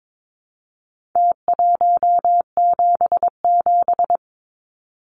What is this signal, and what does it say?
Morse code, heard as a keyed tone, about 22 wpm, 700 Hz. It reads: T177
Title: Numbers - Contest Style - 4-Digit 22wpm